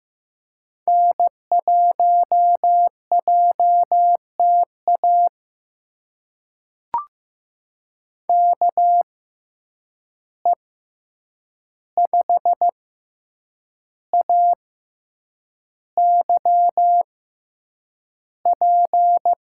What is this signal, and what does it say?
Morse code, heard as a keyed tone, about 15 wpm, spaced as for 7 wpm, 700 Hz. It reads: N1JTA KE5AYP
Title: USA Call Signs - 6x Character Spacing - 15wpm